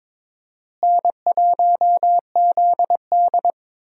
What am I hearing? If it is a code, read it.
Morse code, heard as a keyed tone, about 21 wpm, 700 Hz. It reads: N1ZD
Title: USA Call Signs - 4 Characters - Repeated 2x - 22wpm